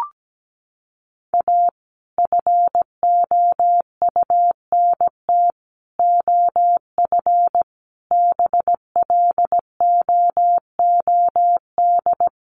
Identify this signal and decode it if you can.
Morse code, heard as a keyed tone, about 17 wpm, 700 Hz. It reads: A FOUNT OF BLOOD
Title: A Sound Of Thunder 17wpm